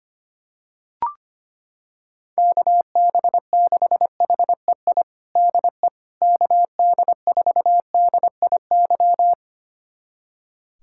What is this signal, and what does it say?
Morse code, heard as a keyed tone, about 25 wpm, 700 Hz. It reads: KB6HEI DE KD4DIY
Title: USA Callsign DE Callsign - 6 Characters - 25wpm